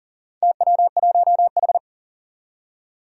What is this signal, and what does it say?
Morse code, heard as a keyed tone, about 40 wpm, 700 Hz. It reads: TW1H
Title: Four Challenging Letter-Number Combinations 40wpm